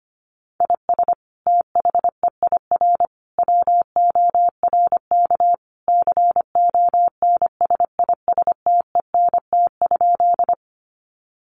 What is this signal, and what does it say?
Morse code, heard as a keyed tone, about 25 wpm, 700 Hz. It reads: IS THEIR WORK CONSISTENT?
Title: Sentences from Top 1500 Words 25wpm